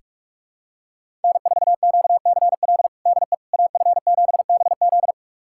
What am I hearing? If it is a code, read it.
Morse code, heard as a keyed tone, about 42 wpm, 700 Hz. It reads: N4QCL DE AF7BZ
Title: USA Callsign DE Callsign - 5 Characters - 45wpm